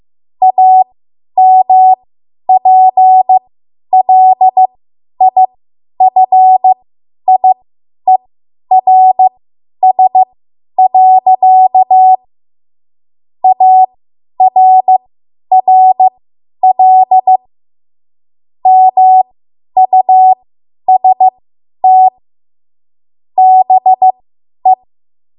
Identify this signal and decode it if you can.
Morse code, heard as a keyed tone, about 15 wpm, spaced as for 10 wpm, 750 Hz. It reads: AMPLIFIERS. ARRL MUST BE
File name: Morse / 10WPM